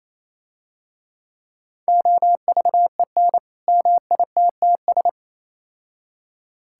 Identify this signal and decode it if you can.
Morse code, heard as a keyed tone, about 27 wpm, 700 Hz. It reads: OVEN MITTS
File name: Easy Words - Speed-Racing with Hints - 22wpm